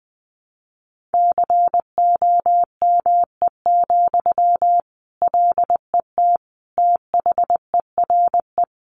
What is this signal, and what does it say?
Morse code, heard as a keyed tone, about 20 wpm, 700 Hz. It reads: COME, LET THERE
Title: Sentences from Top 1500 Words 20wpm